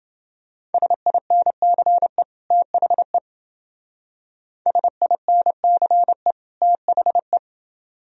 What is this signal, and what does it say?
Morse code, heard as a keyed tone, about 28 wpm, 700 Hz. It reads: SINCE THE SINCE THE